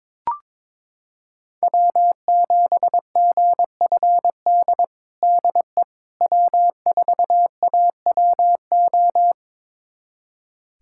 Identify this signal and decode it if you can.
Morse code, heard as a keyed tone, about 22 wpm, 700 Hz. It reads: W7GFD DE W4AWO
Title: USA Callsign DE Callsign - 5 Characters - 22wpm